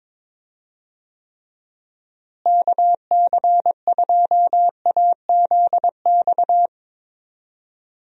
Morse code, heard as a keyed tone, about 22 wpm, 700 Hz. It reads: KC2AZX
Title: USA Call Signs - 6 Characters - Repeated 3x - 22wpm